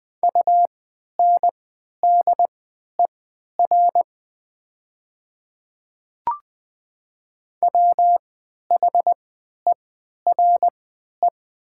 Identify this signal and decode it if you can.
Morse code, heard as a keyed tone, about 20 wpm, spaced as for 11 wpm, 700 Hz. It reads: UNDER WHERE
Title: 3x Character Spacing - Top 200 Words - 20wpm